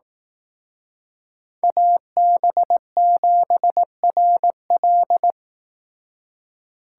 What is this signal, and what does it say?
Morse code, heard as a keyed tone, about 18 wpm, 700 Hz. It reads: AB7RL
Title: Speed Racing - USA Call Signs 15wpm